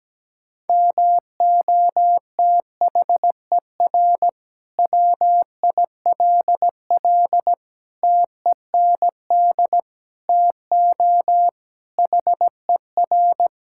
Morse code, heard as a keyed tone, about 17 wpm, 700 Hz. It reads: MOTHER WILL TEND TO HER
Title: Sentences from Top 1000 Words - Review Part 5 - 17wpm